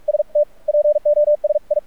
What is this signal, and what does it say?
Morse code, heard as a keyed tone, about 46 wpm, spaced as for 40 wpm, 590 Hz. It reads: ST POSI
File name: archived / CRSnet